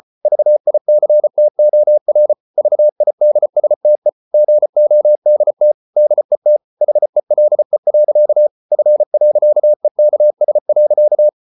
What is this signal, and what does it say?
Morse code, heard as a keyed tone, about 34 wpm, 600 Hz. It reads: VICTOR VIDSTE GODT DET HELE. F.EKS.
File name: Ferie QRQ 34wpm